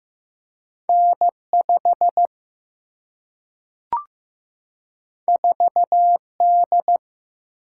Morse code, heard as a keyed tone, about 15 wpm, 700 Hz. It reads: N 5 4 D
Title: Single-Letter Single-Number 15wpm